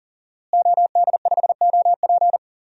Morse code, heard as a keyed tone, about 40 wpm, 700 Hz. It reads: OD5OP
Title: DXCC-100 Call Signs - 5 Characters - Repeated 2x - 40wpm